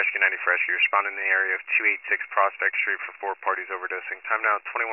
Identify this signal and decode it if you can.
Emergency Alert System (SAME) tones, heard as harsh buzzing bursts, with speech. It is heard from 1.1 to 4.2 s.